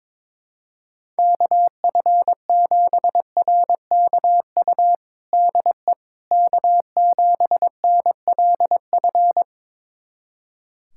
Morse code, heard as a keyed tone, about 22 wpm, 700 Hz. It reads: KF7RKU DE K7NLF
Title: USA Callsign DE Callsign 22wpm